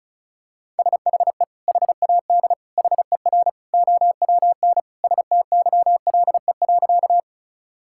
Morse code, heard as a keyed tone, about 36 wpm, 700 Hz. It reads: SHE HAD HER OWN STYLE.
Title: Sentences from Top 1000 Words - Review Part 5 - 35wpm